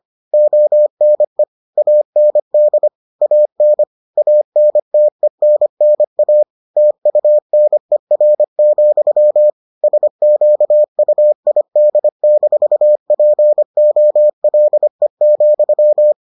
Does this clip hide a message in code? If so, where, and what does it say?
Yes: Morse code from 0.3 to 16.1 s, heard as a keyed tone, about 25 wpm, 600 Hz. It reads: ONE AND AN ANTENNA TUNER, SQUID-POLE,